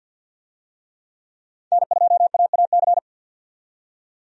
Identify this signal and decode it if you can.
Morse code, heard as a keyed tone, about 49 wpm, 700 Hz. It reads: N2AAC
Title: USA Call Signs - 5 Characters - 50wpm